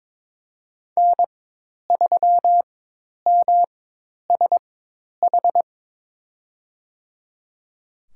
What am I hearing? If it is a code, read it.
Morse code, heard as a keyed tone, about 22 wpm, spaced as for 10 wpm, 700 Hz. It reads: N3MSH
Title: USA Call Signs - 5 Characters - 4x Character Spacing - 22wpm